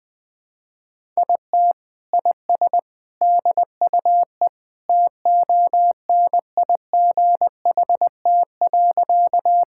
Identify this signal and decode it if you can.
Morse code, heard as a keyed tone, about 20 wpm, 700 Hz. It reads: IT IS DUE TONIGHT.
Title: Sentences from Top 1000 Words - Review Part 6 - 20wpm